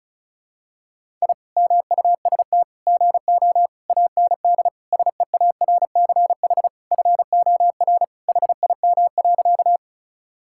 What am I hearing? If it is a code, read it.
Morse code, heard as a keyed tone, about 34 wpm, 700 Hz. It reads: I MUST GO AND SEARCH FOR HIM.
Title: Sentences from Top 1000 Words - Review Part 5 - 35wpm